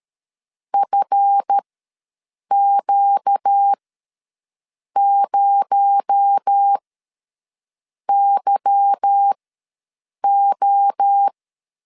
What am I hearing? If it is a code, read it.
Morse code, heard as a keyed tone, about 13 wpm, spaced as for 6 wpm, 800 Hz. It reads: FQ0YO
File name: LEZIONI / 8 YJ45 / 8 YJ ws 12wpm 3342